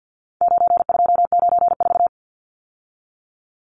Morse code, heard as a keyed tone, about 50 wpm, 700 Hz. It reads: 9294
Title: Numbers - 4-Digit 50wpm